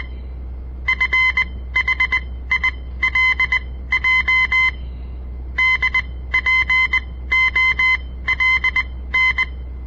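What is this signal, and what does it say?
Morse code, heard as a keyed tone, about 20 wpm, spaced as for 15 wpm, 2050 Hz. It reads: FHILJ DPOLN
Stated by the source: On-air Morse code practice broadcast by a southern Montana amateur radio association; all code, no read back.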